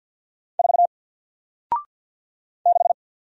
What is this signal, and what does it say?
Morse code, heard as a keyed tone, about 48 wpm, 700 Hz. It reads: 4 6
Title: Single Character (Letter:Number:Punctuation:Prosign) Alt 50wpm